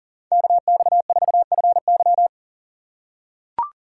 Morse code, heard as a keyed tone, about 40 wpm, 700 Hz. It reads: KX4FY
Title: USA Call Signs - 5 Characters - 2x Character Spacing - 40wpm